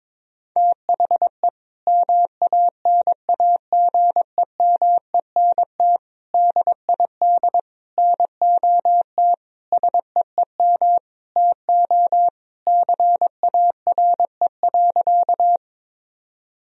Morse code, heard as a keyed tone, about 22 wpm, 700 Hz. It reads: THE MANAGEMENT DID NOT SEEM TO CARE.
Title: Sentences from Top 1000 Words - Review Part 4 - 22wpm